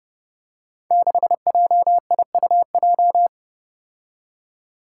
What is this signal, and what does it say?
Morse code, heard as a keyed tone, about 30 wpm, 700 Hz. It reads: 6JIUJ
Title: Five Challenging Letter-Number Combinations 30wpm